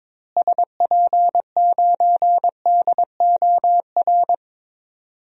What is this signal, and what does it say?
Morse code, heard as a keyed tone, about 22 wpm, 700 Hz. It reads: SP9DOR
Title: DXCC-100 Call Signs - 6 Characters - Repeated 3x - 22wpm